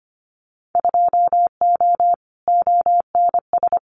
Morse code, heard as a keyed tone, about 25 wpm, 700 Hz. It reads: 2O ONS
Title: Winter Field Day Exchanges Encore 25wpm